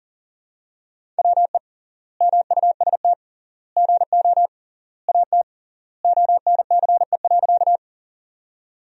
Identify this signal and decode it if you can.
Morse code, heard as a keyed tone, about 40 wpm, 700 Hz. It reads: WE MUST GO AT ONCE.
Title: 3x Word-Spacing - Sentences from Top 300 Words - 40wpm